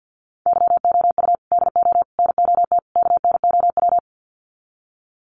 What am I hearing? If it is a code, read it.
Morse code, heard as a keyed tone, about 50 wpm, 700 Hz. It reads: YOU DO NOT KNOW
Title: Sets of 4 Words 50wpm